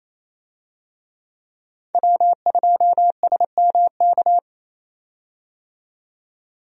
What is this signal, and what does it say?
Morse code, heard as a keyed tone, about 28 wpm, 700 Hz. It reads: W2SMK
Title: USA Call Signs - 5 Characters - 28wpm